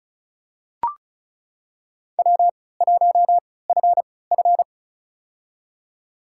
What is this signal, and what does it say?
Morse code, heard as a keyed tone, about 36 wpm, spaced as for 20 wpm, 700 Hz. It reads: W1FF
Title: USA Call Signs - 4 Characters - 3x Character Spacing - 35wpm